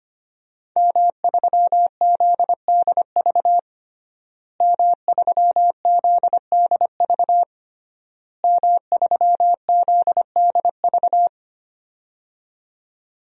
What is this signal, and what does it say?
Morse code, heard as a keyed tone, about 25 wpm, 700 Hz. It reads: M3ZDV M3ZDV M3ZDV
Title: DXCC-100 Call Signs - 5 Characters - Repeated 3x - 25wpm